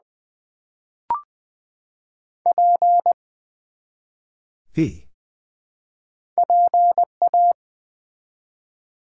Morse code, heard as a keyed tone, about 20 wpm, 700 Hz. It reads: P PA
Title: Build A Word - 1 Letter At A Time - 3 to 5-Letter Words - 20wpm